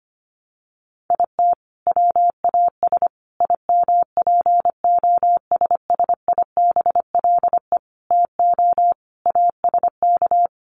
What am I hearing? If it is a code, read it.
Morse code, heard as a keyed tone, about 25 wpm, 700 Hz. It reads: IT WAS IMPOSSIBLE TO ASK
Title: Sentences from Top 1100 Words 25wpm